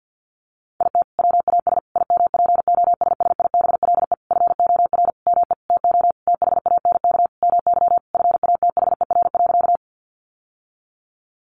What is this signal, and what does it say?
Morse code, heard as a keyed tone, about 50 wpm, 700 Hz. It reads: IT WAS IMPOSSIBLE FOR ME TO THANK MY FATHER.
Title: Sentences from Top 1100 Words 50wpm